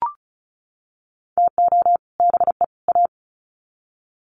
Morse code, heard as a keyed tone, about 34 wpm, 700 Hz. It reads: TO BE A
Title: Sets of 3 Words 35wpm